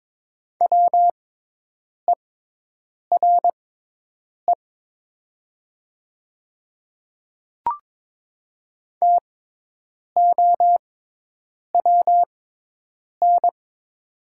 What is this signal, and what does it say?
Morse code, heard as a keyed tone, about 22 wpm, spaced as for 8 wpm, 700 Hz. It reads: WERE TOWN
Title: 6x Character Spacing - Top 500 Words - 22wpm